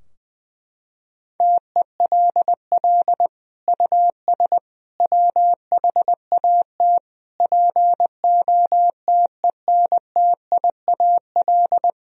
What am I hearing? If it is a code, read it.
Morse code, heard as a keyed tone, about 20 wpm, 700 Hz. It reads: TELL US WHAT POTENTIAL
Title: Sentences from Top 1000 Words - Review Part 4 - 20wpm